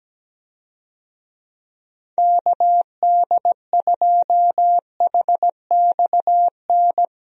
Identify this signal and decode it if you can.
Morse code, heard as a keyed tone, about 17 wpm, 700 Hz. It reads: KD2HXN
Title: USA Call Signs - 6 Characters - Repeated 3x - 17wpm